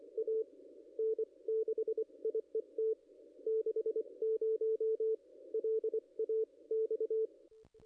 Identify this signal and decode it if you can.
Morse code, heard as a keyed tone, about 24 wpm, spaced as for 21 wpm, 440 Hz. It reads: A N6IET 60 LAX
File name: contests / ss-cw-k5tr-op-n5rz -0200